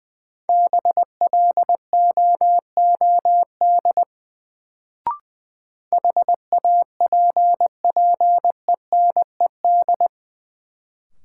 Morse code, heard as a keyed tone, about 20 wpm, 700 Hz. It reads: BLOOD HAPPENED
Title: Sentences from Top 1000 Words - Review Part 2 - New Word Drill - 20wpm